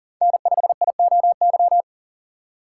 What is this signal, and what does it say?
Morse code, heard as a keyed tone, about 40 wpm, 700 Hz. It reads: N5IOY
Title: USA Call Signs - 5 Characters - 2x Character Spacing - 40wpm